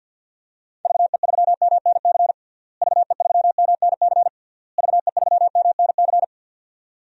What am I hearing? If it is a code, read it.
Morse code, heard as a keyed tone, about 46 wpm, 700 Hz. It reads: VE3MNC VE3MNC VE3MNC
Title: DXCC-100 Call Signs - 6 Characters - Repeated 3x - 50wpm